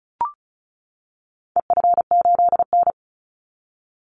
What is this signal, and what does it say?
Morse code, heard as a keyed tone, about 34 wpm, 700 Hz. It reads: EF8N
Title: Call Signs - 35wpm